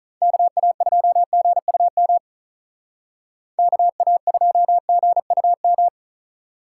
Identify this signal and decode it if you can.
Morse code, heard as a keyed tone, about 44 wpm, 700 Hz. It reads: KA2GUM KA2GUM